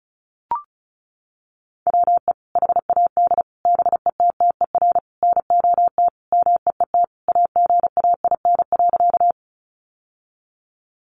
Morse code, heard as a keyed tone, about 34 wpm, 700 Hz. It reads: WE HAD BETTER NOT MEET AGAIN.
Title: Sentences from Top 1100 Words 35wpm